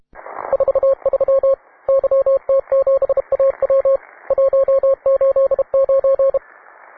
Morse code, heard as a keyed tone, about 31 wpm, 540 Hz, with radio noise. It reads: 43 YT7AW 189